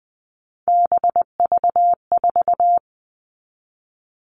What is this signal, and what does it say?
Morse code, heard as a keyed tone, about 20 wpm, 700 Hz. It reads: BV4
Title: All Cut Numbers - 3-Digit 20wpm